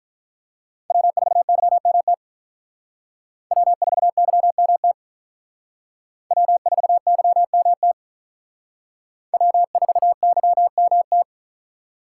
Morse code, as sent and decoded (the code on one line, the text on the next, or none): .-- ....- -.-- -- - / .-- ....- -.-- -- - / .-- ....- -.-- -- - / .-- ....- -.-- -- -
W4YMT W4YMT W4YMT W4YMT